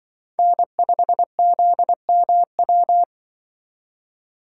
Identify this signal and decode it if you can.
Morse code, heard as a keyed tone, about 24 wpm, 700 Hz. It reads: N5ZMW
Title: Speed Racing - USA 1x3 Call Signs 20wpm